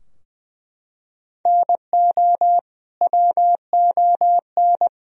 Morse code, heard as a keyed tone, about 21 wpm, 700 Hz. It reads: NO WON
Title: Sentences from Top 1000 Words - Review Part 3 - 20wpm